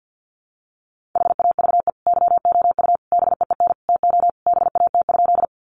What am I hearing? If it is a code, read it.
Morse code, heard as a keyed tone, about 50 wpm, 700 Hz. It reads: HAVE YOU BEEN TO BAT?
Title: Sentences from Top 1500 Words 50wpm